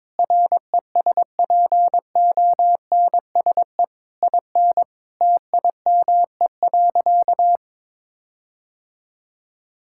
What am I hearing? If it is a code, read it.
Morse code, heard as a keyed tone, about 21 wpm, 700 Hz. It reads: RESPONSE IN TIME.
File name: practice / 22wpm / Sentences with 8-Word Phrases - 22wpm